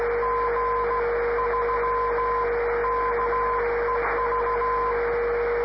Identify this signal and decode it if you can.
Morse code, heard as a keyed tone, about 15 wpm, 1020 Hz, with noise. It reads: G3KEV
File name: 500khz / graberji swl / g3kev